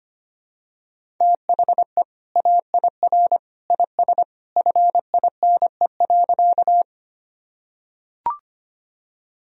Morse code, heard as a keyed tone, about 25 wpm, 700 Hz. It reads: THE AIR IS FINE.
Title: Sentences from Top 1000 Words - Review Part 1 - 25wpm